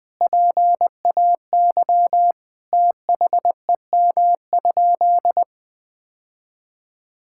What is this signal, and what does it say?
Morse code, heard as a keyed tone, about 20 wpm, 700 Hz. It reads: PAY THEM?
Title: Sentences from Top 1000 Words - Review Part 5 - 20wpm